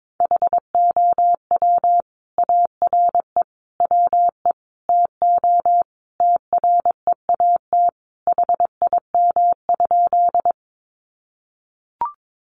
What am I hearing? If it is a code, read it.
Morse code, heard as a keyed tone, about 22 wpm, 700 Hz. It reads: HOW ARE WE TO TREAT HIM?
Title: Sentences from Top 1300 Words 22wpm